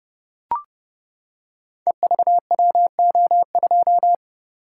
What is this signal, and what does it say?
Morse code, heard as a keyed tone, about 30 wpm, 700 Hz. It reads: EVWO2
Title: Code Groups - 5 Letter:Numbers - 30wpm